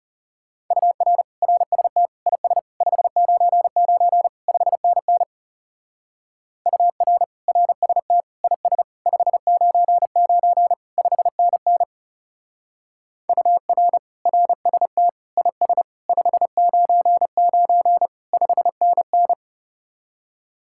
Morse code, heard as a keyed tone, about 35 wpm, 700 Hz. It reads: UR RST IS 599 5NN UR RST IS 599 5NN UR RST IS 599 5NN
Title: Speed-Racing Common QSO Phrases - 30wpm